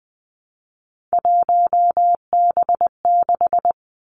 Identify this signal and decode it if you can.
Morse code, heard as a keyed tone, about 20 wpm, 700 Hz. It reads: 1B6
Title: Three Challenging Letter-Number Combinations 20wpm